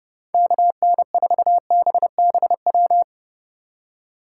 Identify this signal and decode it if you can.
Morse code, heard as a keyed tone, about 36 wpm, 700 Hz. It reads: KN4BBW
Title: Speed Racing - USA 2x3 Call Signs 30wpm